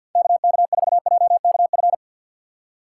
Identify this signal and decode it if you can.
Morse code, heard as a keyed tone, about 46 wpm, 700 Hz. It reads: KK4JKF
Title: USA Call Signs - 6 Characters - 3x Character Spacing - 50wpm